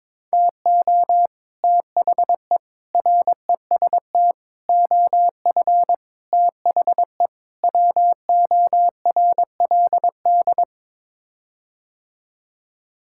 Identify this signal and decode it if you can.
Morse code, heard as a keyed tone, about 22 wpm, 700 Hz. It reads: TO THE REST OF THE WORLD
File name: practice / 22wpm / Sets of 6 Words - 22wpm